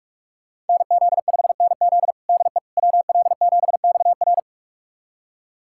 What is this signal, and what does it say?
Morse code, heard as a keyed tone, about 46 wpm, 700 Hz. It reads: NZ5NZ DE WL7XR
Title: USA Callsign DE Callsign - 5 Characters - 45wpm